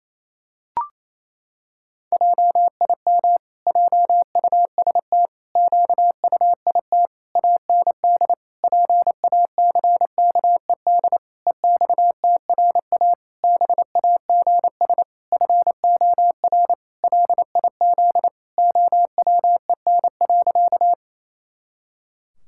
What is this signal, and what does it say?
Morse code, heard as a keyed tone, about 28 wpm, 700 Hz. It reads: JIM JUST QUIT AND PACKED EXTRA BAGS FOR LIZ OWEN.
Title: Pangram Challenge v2 - 28wpm